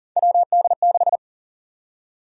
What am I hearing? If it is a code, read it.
Morse code, heard as a keyed tone, about 40 wpm, 700 Hz. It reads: WD6
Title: USA Call Sign Prefixes - Repeated 2x - 40wpm